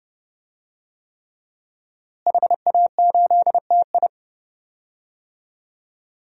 Morse code, heard as a keyed tone, about 31 wpm, 700 Hz. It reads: HA8TI
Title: DXCC-100 Call Signs - 5 Characters - Repeated 3x - 30wpm